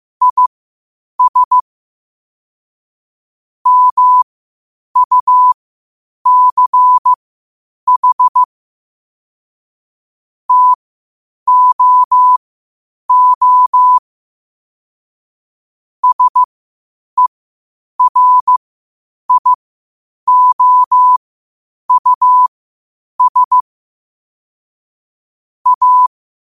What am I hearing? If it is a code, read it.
Morse code, heard as a keyed tone, about 15 wpm, spaced as for 8 wpm, 1000 Hz. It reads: IS MUCH TOO SERIOUS A
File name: qotdfiles / 8wpm